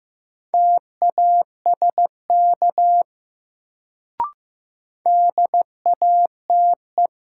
Morse code, heard as a keyed tone, about 15 wpm, 700 Hz. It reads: TASK DATE
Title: Sentences from Top 1000 Words - Review Part 7 - New Word Drill - 15wpm